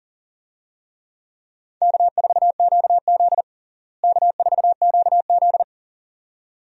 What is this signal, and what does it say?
Morse code, heard as a keyed tone, about 40 wpm, 700 Hz. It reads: K4QZ K4QZ
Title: USA Call Signs - 4 Characters - Repeated 2x - 40wpm